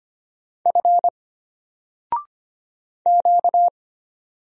Morse code, heard as a keyed tone, about 25 wpm, 700 Hz. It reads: F Q
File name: Single-Letter-Number-25wpm